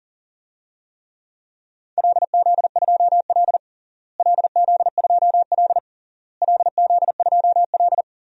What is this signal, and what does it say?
Morse code, heard as a keyed tone, about 40 wpm, 700 Hz. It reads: LZ2L LZ2L LZ2L
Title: DXCC-100 Call Signs - 4 Characters - Repeated 3x - 40wpm